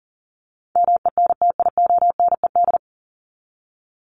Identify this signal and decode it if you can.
Morse code, heard as a keyed tone, about 40 wpm, 700 Hz. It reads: MENTIONED
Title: ED Suffix - Top 100 Words with Base Words 40wpm